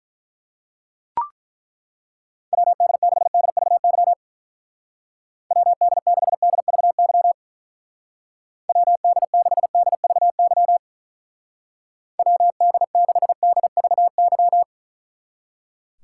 Morse code, heard as a keyed tone, about 42 wpm, 700 Hz. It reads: WD6DVY WD6DVY WD6DVY WD6DVY
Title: Speed Racing - USA 2x3 Call Signs 35wpm